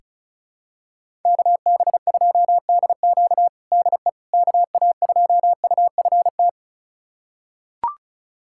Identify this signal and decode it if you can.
Morse code, heard as a keyed tone, about 36 wpm, 690 Hz. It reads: KB2DQ DE KA2UFT
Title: USA Callsign DE Callsign 35wpm